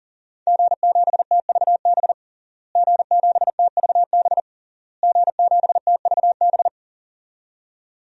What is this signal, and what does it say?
Morse code, heard as a keyed tone, about 40 wpm, 700 Hz. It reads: G7TVB G7TVB G7TVB
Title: DXCC-100 Call Signs - 5 Characters - Repeated 3x - 40wpm